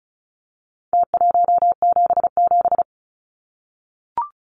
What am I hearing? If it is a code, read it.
Morse code, heard as a keyed tone, about 34 wpm, 700 Hz. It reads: T177
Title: Numbers - Contest Style - 4-Digit 35wpm